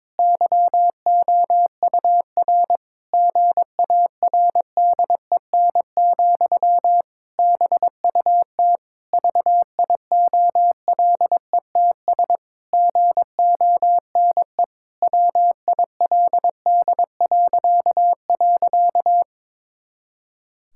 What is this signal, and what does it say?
Morse code, heard as a keyed tone, about 22 wpm, 700 Hz. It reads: YOUR GARDEN, BUT VIOLETS GONE WILD..
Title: Pangram Challenge v2 - 22wpm